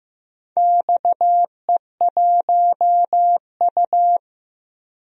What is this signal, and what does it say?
Morse code, heard as a keyed tone, about 15 wpm, 700 Hz. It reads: XE1U
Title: DXCC-100 Call Signs - 4 Characters - 15wpm